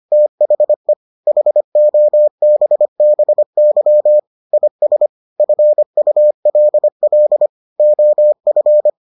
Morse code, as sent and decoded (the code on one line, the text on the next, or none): - .... / .... --- -... -... -.-- / .. ... / ..-. ..- .-.. .-.. / --- ..-.
THE HOBBY IS FULL OF